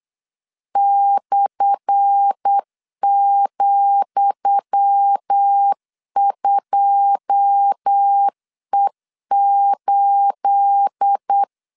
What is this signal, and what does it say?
Morse code, heard as a keyed tone, about 8 wpm, 800 Hz. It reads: /,2E8
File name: LEZIONI / 10 SIGNS / 10D COMA 8wpm 3749